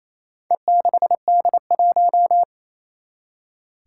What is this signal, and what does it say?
Morse code, heard as a keyed tone, about 27 wpm, 700 Hz. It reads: E6D1
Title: Four Challenging Letter-Number Combinations 28wpm